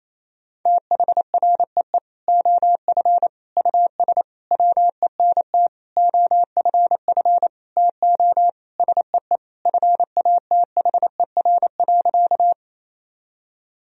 Morse code, heard as a keyed tone, about 28 wpm, 700 Hz. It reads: THREE OF US WENT OFF TO SEE FATHER.
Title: Sentences from Top 1000 Words - Review Part 1 - 28wpm